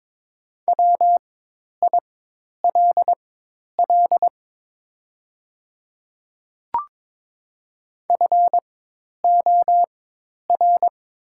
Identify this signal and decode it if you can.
Morse code, heard as a keyed tone, about 22 wpm, spaced as for 10 wpm, 700 Hz. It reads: WILL FOR
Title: 4x Character Spacing - Top 200 Words - 22wpm